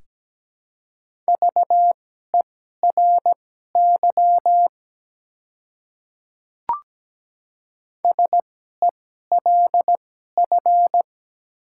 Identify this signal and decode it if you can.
Morse code, heard as a keyed tone, about 17 wpm, spaced as for 12 wpm, 700 Hz. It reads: VERY SELF
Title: 2x Character Spacing - Top 200 Words - 17wpm